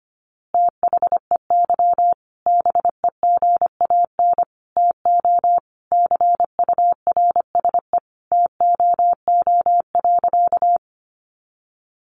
Morse code, heard as a keyed tone, about 25 wpm, 700 Hz. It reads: THEY BEGAN TO CURSE TOO.
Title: A Sound Of Thunder 25wpm